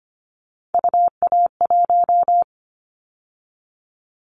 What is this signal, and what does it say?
Morse code, heard as a keyed tone, about 25 wpm, 700 Hz. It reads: UA1
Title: Callsign Prefixes for Top 50 DXCC - 25wpm